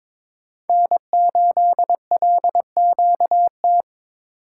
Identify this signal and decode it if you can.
Morse code, heard as a keyed tone, about 22 wpm, 700 Hz. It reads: N8LQT
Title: USA Call Signs - 5 Characters - 2x Character Spacing - 22wpm